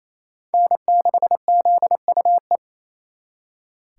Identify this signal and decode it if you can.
Morse code, heard as a keyed tone, about 36 wpm, 700 Hz. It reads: N6ZUE